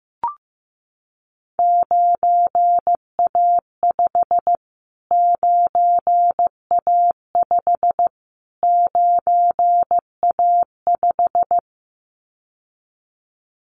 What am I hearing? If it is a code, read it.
Morse code, heard as a keyed tone, about 15 wpm, 700 Hz. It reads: 9A5 9A5 9A5
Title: Callsign Prefixes for Top 25 DXCC - Repeated 3x - 15wpm